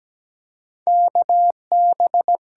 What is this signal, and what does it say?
Morse code, heard as a keyed tone, about 17 wpm, 700 Hz. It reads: KB
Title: USA Call Signs - 8x Character Spacing - 17wpm